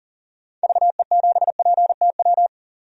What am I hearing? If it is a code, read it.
Morse code, heard as a keyed tone, about 40 wpm, 700 Hz. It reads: VE7PTW
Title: DXCC-100 Call Signs - 6 Characters - 40wpm